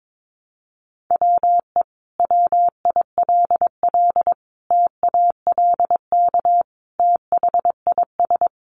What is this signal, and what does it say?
Morse code, heard as a keyed tone, about 22 wpm, 700 Hz. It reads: WE WILL TALK THIS
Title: Sentences from Top 700 Words 22wpm